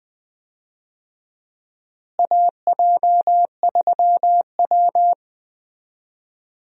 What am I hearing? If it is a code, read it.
Morse code, heard as a keyed tone, about 20 wpm, 700 Hz. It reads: AJ3W
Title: USA Call Signs - 4 Characters - Repeated 2x - 20wpm